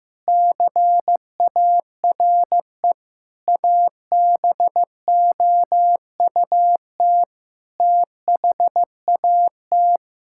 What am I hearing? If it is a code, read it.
Morse code, heard as a keyed tone, about 15 wpm, 700 Hz. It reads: CARE ABOUT THAT
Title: Sentences from Top 1000 Words - Review Part 6 - 15wpm